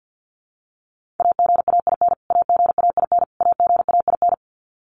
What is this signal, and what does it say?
Morse code, heard as a keyed tone, about 50 wpm, 700 Hz. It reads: AGAIN AGAIN AGAIN
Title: Top 200 Words Repeated 50wpm